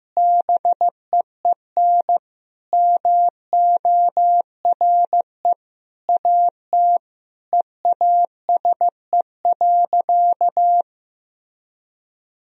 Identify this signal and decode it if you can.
Morse code, heard as a keyed tone, about 15 wpm, 700 Hz. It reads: BEEN MORE AT EASE.
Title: Sentences from Top 1000 Words - Review Part 1 - 15wpm